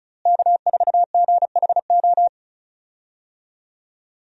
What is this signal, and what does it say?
Morse code, heard as a keyed tone, about 36 wpm, 700 Hz. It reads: K4GHO
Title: USA Call Signs - 5 Characters - Repeated 2x - 35wpm